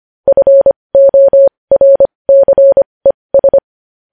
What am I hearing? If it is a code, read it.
Morse code, heard as a keyed tone, about 25 wpm, spaced as for 20 wpm, 550 Hz. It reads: FORCES
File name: podcast / News headlines-1x-25x20